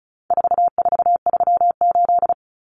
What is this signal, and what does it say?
Morse code, heard as a keyed tone, about 34 wpm, 700 Hz. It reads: 4438
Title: Numbers - 4-Digit 35wpm